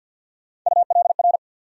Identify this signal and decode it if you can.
Morse code, heard as a keyed tone, about 52 wpm, 700 Hz. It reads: ULR
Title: USA Call Sign Postfixes - 3 Letters 50wpm